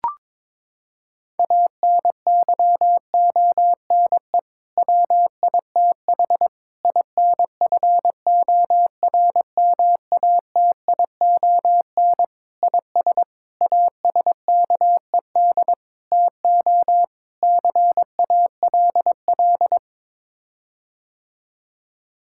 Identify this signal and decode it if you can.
Morse code, heard as a keyed tone, about 22 wpm, 700 Hz. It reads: ANYONE WITH INFORMATION IS ASKED TO CALL
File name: practice / 22wpm / Sets of 7 Words - 22wpm